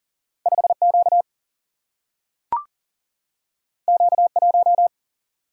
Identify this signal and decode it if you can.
Morse code, heard as a keyed tone, about 41 wpm, 700 Hz. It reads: 5Q Q1
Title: Single-Letter Single-Number 40wpm